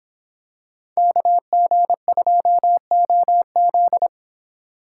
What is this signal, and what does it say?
Morse code, heard as a keyed tone, about 26 wpm, 700 Hz. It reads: KG2OZ